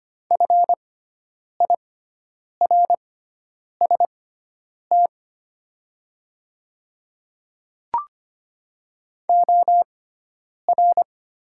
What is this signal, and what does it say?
Morse code, heard as a keyed tone, about 25 wpm, spaced as for 9 wpm, 700 Hz. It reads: FIRST OR
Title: 6x Character Spacing - Top 100 Words - 25wpm